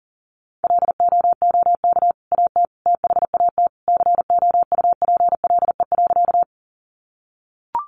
Morse code, heard as a keyed tone, about 40 wpm, 700 Hz. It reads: LOOK AT THAT COUPLE.
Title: Sentences from Top 1100 Words 40wpm